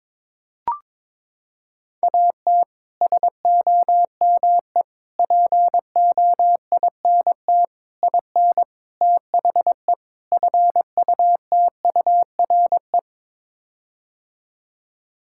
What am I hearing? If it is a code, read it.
Morse code, heard as a keyed tone, about 22 wpm, 700 Hz. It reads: AT SOME POINT IN THE FUTURE
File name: practice / 22wpm / Sets of 6 Words - 22wpm